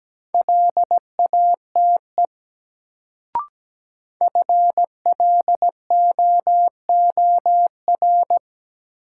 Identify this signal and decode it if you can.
Morse code, heard as a keyed tone, about 17 wpm, 700 Hz. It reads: LATE FLOOR
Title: Sets of 3 Words - New Word Drill - 17wpm